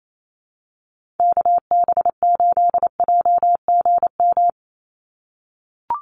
Morse code, heard as a keyed tone, about 28 wpm, 700 Hz. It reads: KB8JGM
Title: USA Call Signs 28wpm